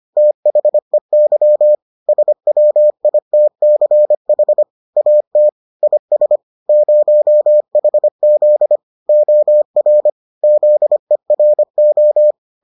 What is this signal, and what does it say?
Morse code, heard as a keyed tone, about 25 wpm, 600 Hz. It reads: THEY SWITCH AT IS 0HZ OR ZERO